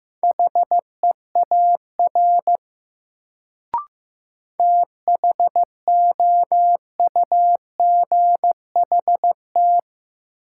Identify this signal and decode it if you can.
Morse code, heard as a keyed tone, about 15 wpm, 700 Hz. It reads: HEAR THOUGHT
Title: Sentences from Top 300 Words - New Word Drill - 15wpm